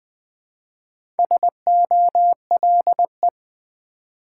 Morse code, heard as a keyed tone, about 20 wpm, 700 Hz. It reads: SOLE
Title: Build A Word - 2 Letters At A Time - 4-Letter Words - 20wpm